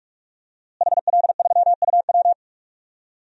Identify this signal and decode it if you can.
Morse code, heard as a keyed tone, about 46 wpm, 700 Hz. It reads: HL3UW
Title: DXCC-100 Call Signs - 5 Characters - Repeated 2x - 45wpm